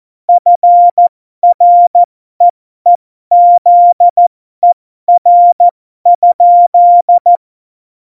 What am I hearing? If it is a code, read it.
Morse code, heard as a keyed tone, about 14 wpm, spaced as for 12 wpm, 700 Hz. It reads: FREEZER?
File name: Morsekurs / Witze